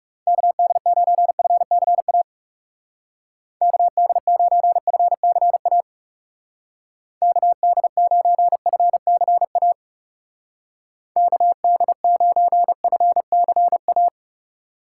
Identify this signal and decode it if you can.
Morse code, heard as a keyed tone, about 38 wpm, 700 Hz. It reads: KD9FCA KD9FCA KD9FCA KD9FCA